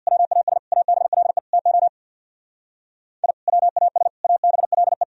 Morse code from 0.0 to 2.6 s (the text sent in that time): WAS ABLE TO